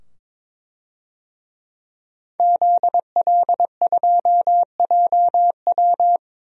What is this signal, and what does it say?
Morse code, heard as a keyed tone, about 22 wpm, 700 Hz. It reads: ZL2JW
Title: DXCC-100 Call Signs - 5 Characters - 22wpm